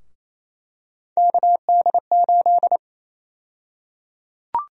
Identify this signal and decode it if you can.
Morse code, heard as a keyed tone, about 28 wpm, 700 Hz. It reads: KD8
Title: USA Call Sign Prefixes - Repeated 2x - 28wpm